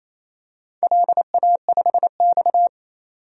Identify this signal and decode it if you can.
Morse code, heard as a keyed tone, about 28 wpm, 700 Hz. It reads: LA5X
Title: DXCC-100 Call Signs - 4 Characters - 28wpm